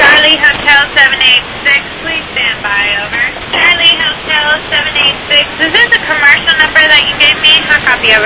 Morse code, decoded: none

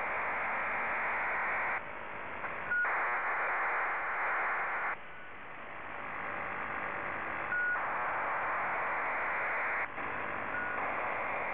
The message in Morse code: none

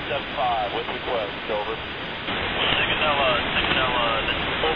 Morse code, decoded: none

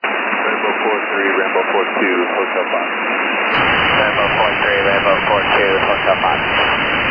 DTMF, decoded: none

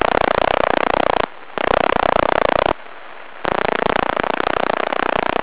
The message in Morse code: none

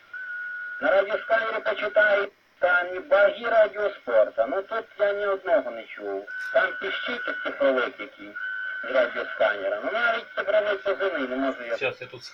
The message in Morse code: none

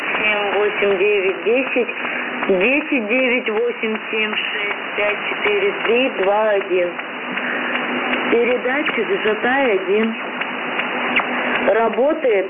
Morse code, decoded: none

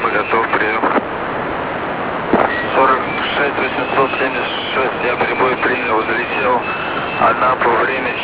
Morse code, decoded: none

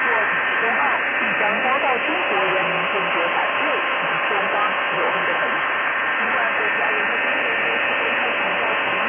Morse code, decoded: none

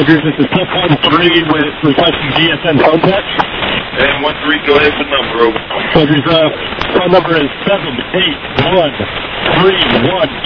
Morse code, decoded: none

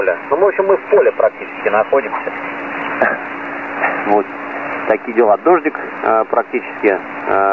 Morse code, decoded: none

